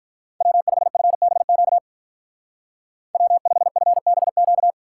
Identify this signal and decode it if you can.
Morse code, heard as a keyed tone, about 44 wpm, 700 Hz. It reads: W5FBQ W5FBQ
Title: Speed Racing - USA 1x3 Call Signs 35wpm